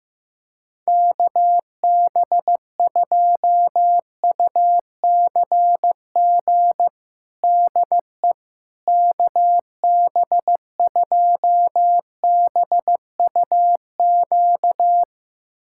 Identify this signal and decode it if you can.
Morse code, heard as a keyed tone, about 16 wpm, 700 Hz. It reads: KB2UCG DE KB2BUQ
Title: USA Callsign DE Callsign - 6 Characters - 15wpm